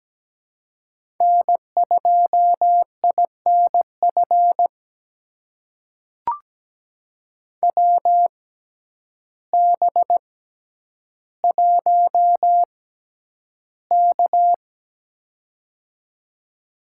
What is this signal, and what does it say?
Morse code, heard as a keyed tone, about 17 wpm, spaced as for 9 wpm, 700 Hz. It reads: N2INF WB1K
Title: USA Call Signs - 6x Character Spacing - 17wpm